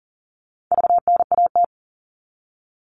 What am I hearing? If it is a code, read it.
Morse code, heard as a keyed tone, about 40 wpm, 700 Hz. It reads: VNAT
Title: All Cut Numbers - 4-Digit 40wpm